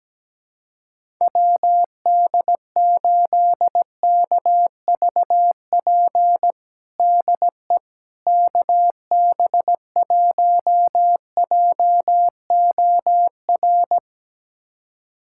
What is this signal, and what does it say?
Morse code, heard as a keyed tone, about 17 wpm, 700 Hz. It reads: WD8KVP DE KB1JOR
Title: USA Callsign DE Callsign - 6 Characters - 17wpm